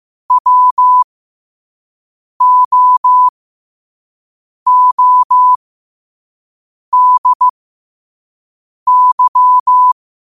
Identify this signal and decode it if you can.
Morse code, heard as a keyed tone, about 15 wpm, spaced as for 5 wpm, 1000 Hz. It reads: WOODY
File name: qotdfiles / 5wpm